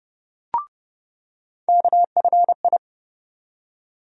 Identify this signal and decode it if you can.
Morse code, heard as a keyed tone, about 31 wpm, 700 Hz. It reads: KFI
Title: USA Call Sign Postfixes - 3 Letters 30wpm